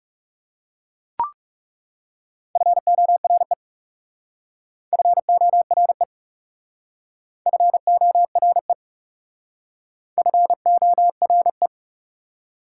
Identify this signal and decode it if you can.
Morse code, heard as a keyed tone, about 37 wpm, 690 Hz. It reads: FORE FORE FORE FORE
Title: Speed-Racing Prefixes and Suffixes 30wpm